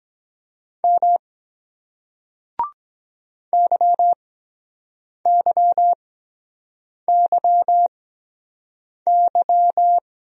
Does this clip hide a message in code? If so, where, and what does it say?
Yes: Morse code from 0.8 to 10.0 s, heard as a keyed tone, about 23 wpm, spaced as for 7 wpm, 700 Hz. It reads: M YYYY